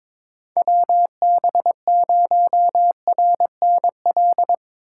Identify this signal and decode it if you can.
Morse code, heard as a keyed tone, about 22 wpm, 700 Hz. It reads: WB0RNL
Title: USA Call Signs - 6 Characters - 2x Character Spacing - 22wpm